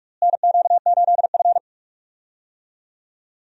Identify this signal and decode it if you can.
Morse code, heard as a keyed tone, about 45 wpm, 700 Hz. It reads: NQ8F
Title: USA Call Signs - 4 Characters - Repeated 3x - 45wpm